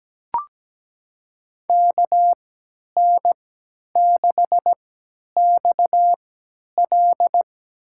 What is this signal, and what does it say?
Morse code, heard as a keyed tone, about 17 wpm, spaced as for 10 wpm, 700 Hz. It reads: KN6XL
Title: USA Call Signs - 5 Characters - 3x Character Spacing - 17wpm